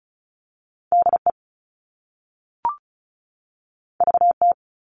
Morse code, heard as a keyed tone, about 36 wpm, 700 Hz. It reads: DE VT
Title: US States 35wpm